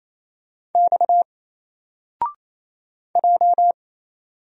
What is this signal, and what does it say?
Morse code, heard as a keyed tone, about 28 wpm, 700 Hz. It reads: X J
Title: Single Letter-Number Alt 28wpm